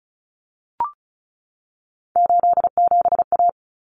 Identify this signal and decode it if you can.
Morse code, heard as a keyed tone, about 34 wpm, 700 Hz. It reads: 87A
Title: Common Cut Numbers - 3-Digit 35wpm